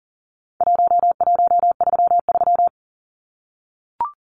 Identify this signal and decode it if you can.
Morse code, heard as a keyed tone, about 40 wpm, 700 Hz. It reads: 1133
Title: Numbers - Contest Style - 4-Digit 40wpm